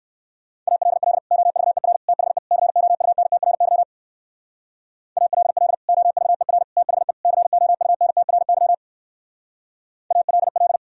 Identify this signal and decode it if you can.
Morse code, heard as a keyed tone, about 59 wpm, 700 Hz. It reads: ALL OVER THE COUNTRY ALL OVER THE COUNTRY ALL